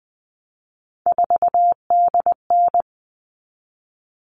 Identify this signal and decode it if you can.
Morse code, heard as a keyed tone, about 20 wpm, 700 Hz. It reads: 4DN
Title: All Cut Numbers - 3-Digit 20wpm